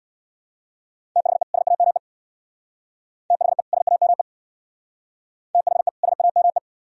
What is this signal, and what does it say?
Morse code, heard as a keyed tone, about 61 wpm, 700 Hz. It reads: THE SAME THE SAME THE SAME
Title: Speed-Racing - Sets of 2 Words 50wpm